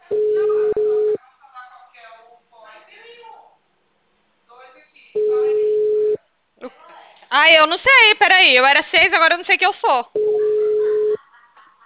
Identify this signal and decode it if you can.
Ringback tone (Europe, 425 Hz), with speech.
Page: Ribeirao sala 906